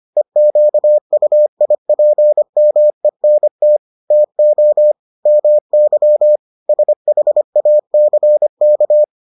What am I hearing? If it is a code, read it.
Morse code, heard as a keyed tone, about 25 wpm, 600 Hz. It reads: EQUIPMENT TO MY SHACK